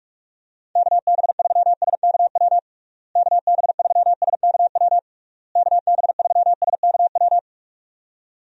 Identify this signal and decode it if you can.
Morse code, heard as a keyed tone, about 41 wpm, 700 Hz. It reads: KB3SKW KB3SKW KB3SKW
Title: USA Call Signs - 6 Characters - Repeated 3x - 45wpm